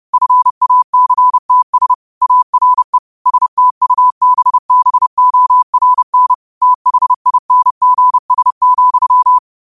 Morse code, heard as a keyed tone, about 30 wpm, 1000 Hz. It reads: FACTS ARE STUBBORN THINGS,